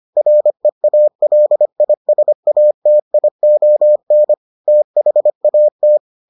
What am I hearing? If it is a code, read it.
Morse code, heard as a keyed tone, about 25 wpm, 600 Hz. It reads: REALISATION THAT